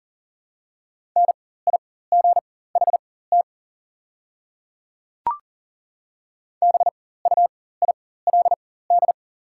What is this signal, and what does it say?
Morse code, heard as a keyed tone, about 37 wpm, spaced as for 18 wpm, 700 Hz. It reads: NIGHT BUILD
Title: 4x Character Spacing - Top 500 Words - 40wpm